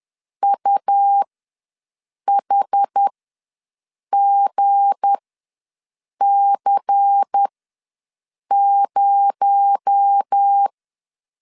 Morse code, heard as a keyed tone, about 10 wpm, spaced as for 6 wpm, 800 Hz. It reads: UHGC0
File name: LEZIONI / 7 GF9Z / 7 G ws 10wpm 3843